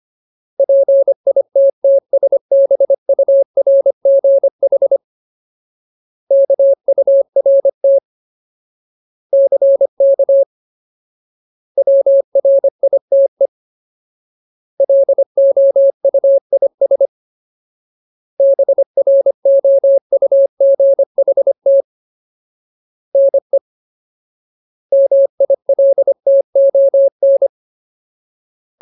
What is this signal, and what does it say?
Morse code, heard as a keyed tone, about 25 wpm, 550 Hz. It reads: PITTSBURGH KURT CK WRITE LOUIS BROUGHT NE MILTON
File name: Ultimate Word List / 25WPM Ultimate